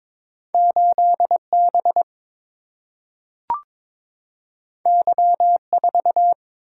Morse code, heard as a keyed tone, about 22 wpm, 700 Hz. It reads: 8 B Y 4
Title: Single-Letter Single-Number 22wpm